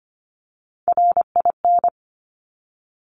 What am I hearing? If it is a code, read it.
Morse code, heard as a keyed tone, about 25 wpm, 700 Hz. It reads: RIN
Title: Top Three Letter Combinations in Words 25wpm